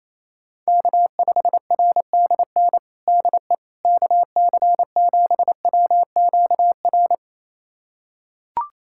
Morse code, heard as a keyed tone, about 28 wpm, 700 Hz. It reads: K5RDN DE KC7WQR
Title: USA Callsign DE Callsign 28wpm